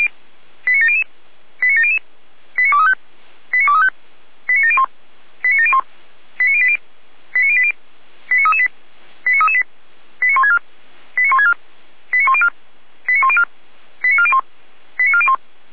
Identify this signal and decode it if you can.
Five-tone selective call (ZVEI), heard as quick runs of five tones, tones of about 70 ms. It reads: 89700 89700 89235 89235 89701 89701 89080 89080 89338 89338 89164 89164 89104 89104 89401 89401
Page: SDR/ZVEI